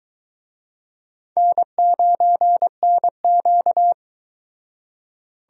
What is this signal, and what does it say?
Morse code, heard as a keyed tone, about 23 wpm, 700 Hz. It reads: N9NQ